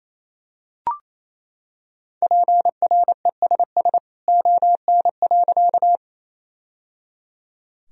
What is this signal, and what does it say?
Morse code, heard as a keyed tone, about 28 wpm, 700 Hz. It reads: PRESS ON.
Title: Sentences from Top 1000 Words - Review Part 1 - 28wpm